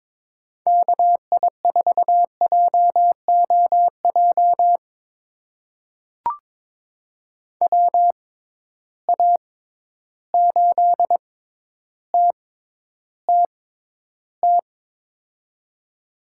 Morse code, heard as a keyed tone, about 22 wpm, spaced as for 11 wpm, 700 Hz. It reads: KI4JOJ WA8TTT
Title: USA Call Signs - 6 Characters - 6x Character Spacing - 22wpm